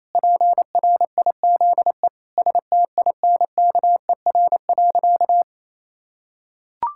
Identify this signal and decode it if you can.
Morse code, heard as a keyed tone, about 28 wpm, 700 Hz. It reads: PRIZE STINKER.
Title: Pangram Challenge v2 - 28wpm